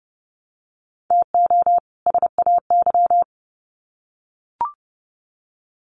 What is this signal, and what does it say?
Morse code, heard as a keyed tone, about 30 wpm, 700 Hz. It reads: TO SAY
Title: Sets of 2 Words 30wpm